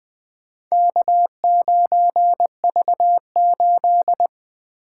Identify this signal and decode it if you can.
Morse code, heard as a keyed tone, about 20 wpm, 700 Hz. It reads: K9V8
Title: Four Letter-Number Combinations - 20wpm